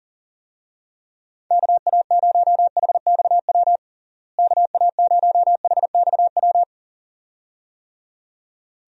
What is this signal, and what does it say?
Morse code, heard as a keyed tone, about 37 wpm, 700 Hz. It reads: KA0HXW KA0HXW
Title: USA Call Signs - 6 Characters - Repeated 2x - 40wpm